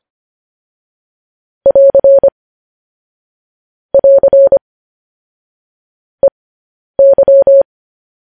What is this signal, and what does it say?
Morse code, heard as a keyed tone, about 25 wpm, spaced as for 10 wpm, 550 Hz. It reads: <AR> <AR> EY
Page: news-headlines-in-morse-code-at-10-wpm